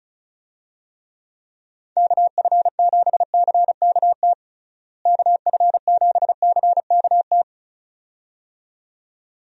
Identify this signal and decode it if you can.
Morse code, heard as a keyed tone, about 36 wpm, 700 Hz. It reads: KF7CKT KF7CKT
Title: USA Call Signs - 6 Characters - Repeated 2x - 35wpm